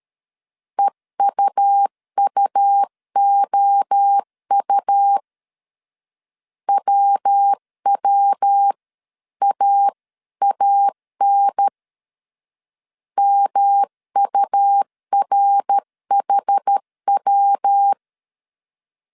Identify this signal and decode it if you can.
Morse code, heard as a keyed tone, about 13 wpm, 800 Hz. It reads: EUUOU WWAAN MURHW
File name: CW / DET / 1 3 1500C S A 12wpm 2905